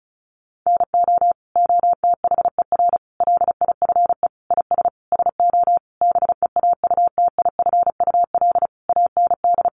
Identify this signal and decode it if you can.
Morse code, heard as a keyed tone, about 34 wpm, 700 Hz. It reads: NO OTHER LIFE IS SO BEAUTIFUL AND
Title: Sentences from Top 1500 Words 35wpm